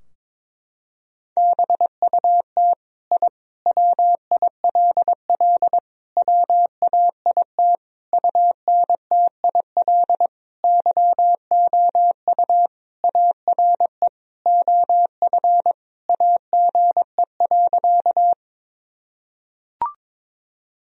Morse code, heard as a keyed tone, about 22 wpm, 700 Hz. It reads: BUT I WILL WAIT UNTIL YOU ARE OF AGE.
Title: Sentences from Top 1000 Words - Review Part 3 - 22wpm